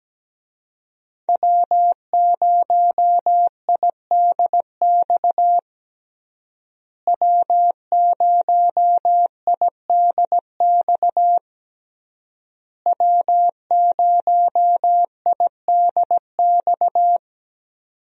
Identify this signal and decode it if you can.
Morse code, heard as a keyed tone, about 17 wpm, 700 Hz. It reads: W0IDX W0IDX W0IDX
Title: USA Call Signs - 5 Characters - Repeated 3x - 17wpm